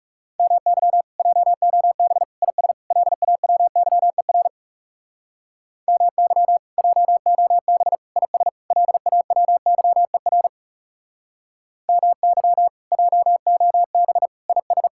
Morse code, heard as a keyed tone, about 37 wpm, 700 Hz. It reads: MY JOB IS LAWYER MY JOB IS LAWYER MY JOB IS
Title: Speed-Racing Common QSO Phrases - 30wpm